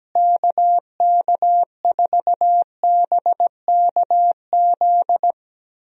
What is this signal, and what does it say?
Morse code, heard as a keyed tone, about 17 wpm, 700 Hz. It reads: KK4BKZ
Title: USA Call Signs - 6 Characters - Repeated 3x - 17wpm